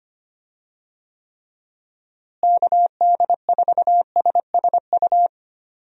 Morse code, heard as a keyed tone, about 25 wpm, 700 Hz. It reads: KD4SSU
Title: USA Call Signs - 6 Characters - Repeated 3x - 25wpm